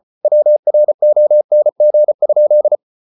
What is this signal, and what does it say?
Morse code, heard as a keyed tone, about 34 wpm, 600 Hz. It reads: WRONG?